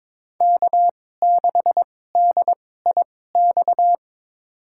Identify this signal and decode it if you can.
Morse code, heard as a keyed tone, about 22 wpm, spaced as for 16 wpm, 700 Hz. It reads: K6DIX
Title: USA Call Signs - 5 Characters - 2x Character Spacing - 22wpm